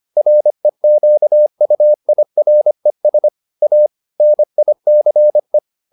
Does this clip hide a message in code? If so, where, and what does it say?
Yes: Morse code from 0.2 to 5.6 s, heard as a keyed tone, about 25 wpm, 600 Hz. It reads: REQUIRES A NICE